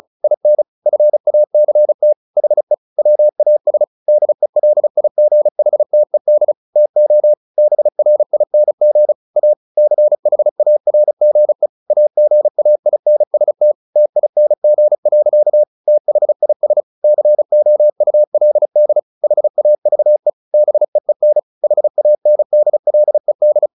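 Morse code, heard as a keyed tone, about 34 wpm, 600 Hz. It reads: IN FACT HE WAS DELIGHTED TO BRING A CHARGE AGAINST TING. THIS COULD HAVE BEEN HANDLED
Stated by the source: Morse practice at 35 wpm from CW med Gnister, sending Ken Follett's novel Never in English.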